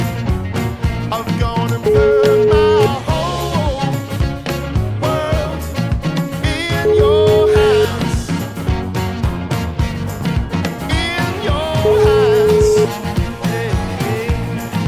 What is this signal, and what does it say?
Ringback tone (Europe, 425 Hz), with noise.